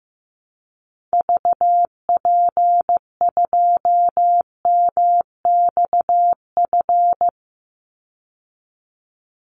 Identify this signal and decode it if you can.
Morse code, heard as a keyed tone, about 15 wpm, 700 Hz. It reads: VP2MXF
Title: Call Signs Repeated 3x - 15wpm